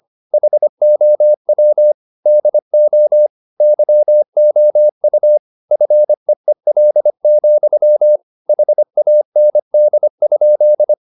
Morse code, heard as a keyed tone, about 25 wpm, 600 Hz. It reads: HOW DO YOU FEEL, HAND?
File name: the old man and the sea part 1 25wpm